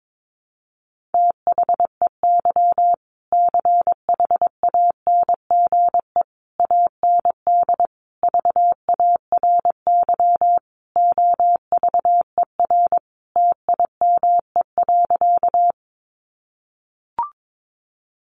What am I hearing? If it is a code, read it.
Morse code, heard as a keyed tone, about 22 wpm, 700 Hz. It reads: THEY CHANGE AND VARY OVER TIME.
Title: Sentences from Top 1200 Words 22wpm